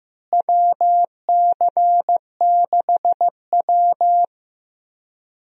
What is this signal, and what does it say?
Morse code, heard as a keyed tone, about 15 wpm, 700 Hz. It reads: WC6W
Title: USA Call Signs - 4 Characters - Repeated 2x - 15wpm